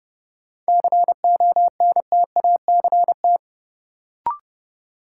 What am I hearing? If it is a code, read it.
Morse code, heard as a keyed tone, about 30 wpm, 700 Hz. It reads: CONTACT
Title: Sentences from Top 1000 Words - Review Part 7 - New Word Drill - 30wpm